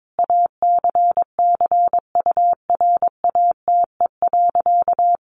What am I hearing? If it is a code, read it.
Morse code, heard as a keyed tone, about 22 wpm, 700 Hz. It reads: ACCURATE.
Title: Sentences from Top 1200 Words 22wpm